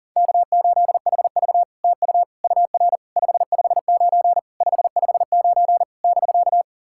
Morse code, heard as a keyed tone, about 40 wpm, 660 Hz. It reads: K8HV TU UR 559 559 <BK>
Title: SOTA QSO Practice 40wpm